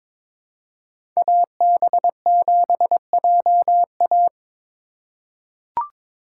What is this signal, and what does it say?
Morse code, heard as a keyed tone, about 22 wpm, 700 Hz. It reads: AB7JA
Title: USA Call Signs - 5 Characters - 2x Character Spacing - 22wpm